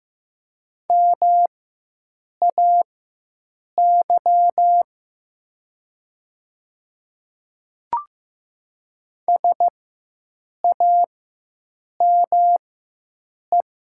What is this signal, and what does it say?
Morse code, heard as a keyed tone, about 15 wpm, spaced as for 7 wpm, 700 Hz. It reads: MAY SAME
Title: 4x Character Spacing - Top 300 Words - 15wpm